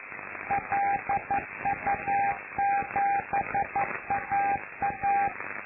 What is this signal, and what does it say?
Morse code, heard as a keyed tone, about 11 wpm, 800 Hz, with noise. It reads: LU7AA
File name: lu4aao / Globo Amsat